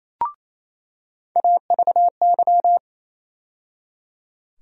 Morse code, heard as a keyed tone, about 28 wpm, 700 Hz. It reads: AVY
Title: USA Call Sign Postfixes - 3 Letters 28wpm